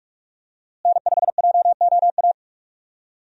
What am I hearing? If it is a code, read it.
Morse code, heard as a keyed tone, about 46 wpm, 700 Hz. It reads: N5JOA
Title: USA Call Signs - 5 Characters - Repeated 3x - 45wpm